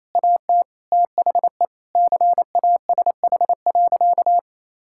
Morse code, heard as a keyed tone, about 28 wpm, 700 Hz. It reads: AT THE CASH.
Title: Sentences from Top 1000 Words - Review Part 5 - 28wpm